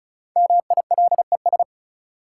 Morse code, heard as a keyed tone, about 36 wpm, 700 Hz. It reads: MILES
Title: Sentences from Top 1000 Words - Review Part 3 - New Word Drill - 35wpm